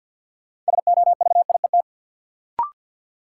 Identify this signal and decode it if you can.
Morse code, heard as a keyed tone, about 53 wpm, 700 Hz. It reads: SOVIET
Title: Sets of 5 Words Encore - New Word Drill - 50wpm